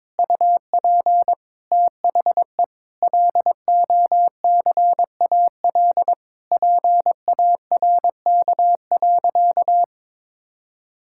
Morse code, heard as a keyed tone, about 22 wpm, 700 Hz. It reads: UP THE LOCAL PARK.
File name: Difficult Sentences with Hints - 22wpm